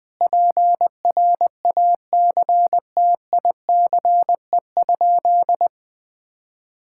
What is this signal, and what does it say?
Morse code, heard as a keyed tone, about 20 wpm, 700 Hz. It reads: PRACTICE?
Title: Sentences from Top 1000 Words - Review Part 6 - 20wpm